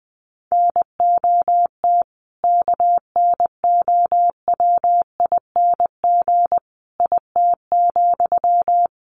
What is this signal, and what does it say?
Morse code, heard as a keyed tone, about 20 wpm, 700 Hz. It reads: NOT KNOWING IT,
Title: A Sound Of Thunder 20wpm